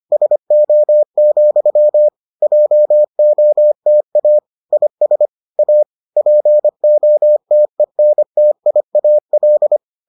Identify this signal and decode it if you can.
Morse code, heard as a keyed tone, about 25 wpm, 600 Hz. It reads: SO, JOTA IS A POTENTIAL